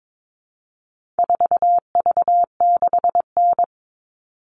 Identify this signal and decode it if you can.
Morse code, heard as a keyed tone, about 22 wpm, 700 Hz. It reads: 4V6N
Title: All Cut Numbers - 4-Digit 22wpm